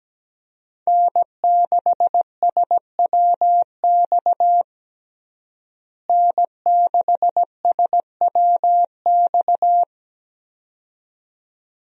Morse code, heard as a keyed tone, about 17 wpm, 700 Hz. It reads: N6SWX N6SWX
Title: USA Call Signs - 5 Characters - Repeated 2x - 17wpm